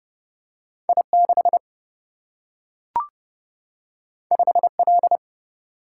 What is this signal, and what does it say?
Morse code, heard as a keyed tone, about 30 wpm, 700 Hz. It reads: I 6 5 L
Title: Single-Letter Single-Number 30wpm